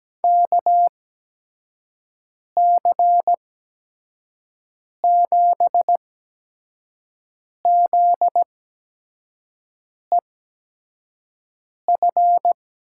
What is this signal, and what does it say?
Morse code, heard as a keyed tone, about 17 wpm, spaced as for 5 wpm, 700 Hz. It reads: KC7ZEF
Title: USA Call Signs - 6 Characters - 8x Character Spacing - 17wpm